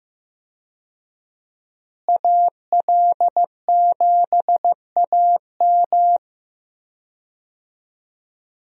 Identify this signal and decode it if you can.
Morse code, heard as a keyed tone, about 15 wpm, 700 Hz. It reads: AL7AM
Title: USA Call Signs - 5 Characters - Repeated 2x - 15wpm